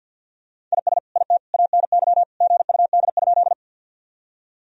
Morse code, heard as a keyed tone, about 49 wpm, 700 Hz. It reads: IS IT ANY GUD?
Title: Sentences from Top 200 Words and Abbreviations 50wpm